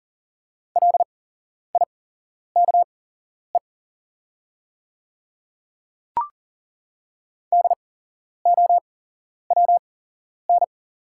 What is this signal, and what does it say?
Morse code, heard as a keyed tone, about 39 wpm, spaced as for 11 wpm, 700 Hz. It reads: LIKE DOWN
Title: 8x Character Spacing - Top 100 Words - 40wpm